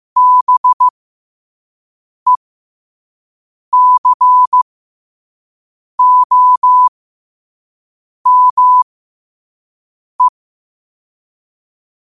Morse code, heard as a keyed tone, about 15 wpm, spaced as for 5 wpm, 1000 Hz. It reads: BECOME
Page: 5 WPM morse code quotes for Wed, 30 Jul 2025.